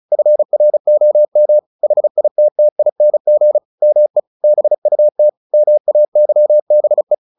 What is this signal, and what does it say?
Morse code, heard as a keyed tone, about 34 wpm, 600 Hz. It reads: FROM HITTING ME BUT MAYBE
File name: the old man and the sea 2 35wpm